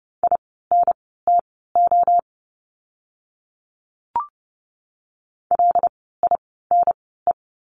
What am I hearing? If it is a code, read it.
Morse code, heard as a keyed tone, about 30 wpm, spaced as for 17 wpm, 700 Hz. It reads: INTO LINE
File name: course / 30:10wpm / 115 Lesson 19f - Letter w - Words 30wpm